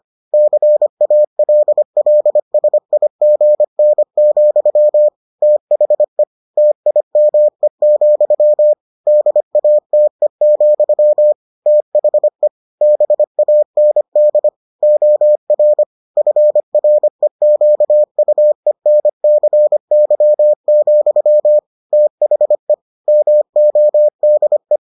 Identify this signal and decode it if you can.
Morse code, heard as a keyed tone, about 25 wpm, 600 Hz. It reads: CALLSIGN, THE TIME, DATE, THE BAND OR FREQUENCY, THE MODE